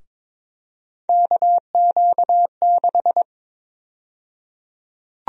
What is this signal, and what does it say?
Morse code, heard as a keyed tone, about 22 wpm, 700 Hz. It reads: KQ6
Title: USA Call Sign Prefixes - Repeated 2x - 22wpm